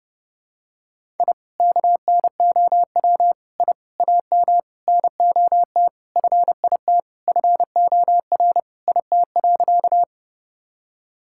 Morse code, heard as a keyed tone, about 30 wpm, 700 Hz. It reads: I KNOW I AM NOT FIT FOR IT.
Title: Sentences from Top 1000 Words - Review Part 5 - 30wpm